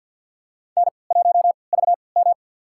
Morse code, heard as a keyed tone, about 52 wpm, spaced as for 29 wpm, 690 Hz. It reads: N1VM
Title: USA Call Signs - 3x Character Spacing - 50wpm